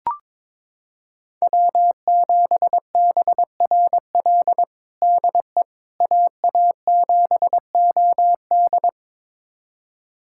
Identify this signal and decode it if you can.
Morse code, heard as a keyed tone, about 22 wpm, 700 Hz. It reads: W7BRL DE AA7OD
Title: USA Callsign DE Callsign - 5 Characters - 22wpm